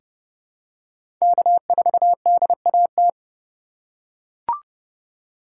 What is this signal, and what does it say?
Morse code, heard as a keyed tone, about 30 wpm, 700 Hz. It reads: K4DAT
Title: USA Call Signs - 3x Character Spacing - 30wpm